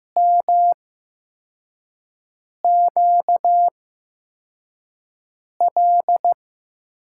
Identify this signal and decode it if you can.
Morse code, heard as a keyed tone, about 15 wpm, spaced as for 4 wpm, 700 Hz. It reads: MQL
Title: USA Call Signs - 8x Character Spacing - 15wpm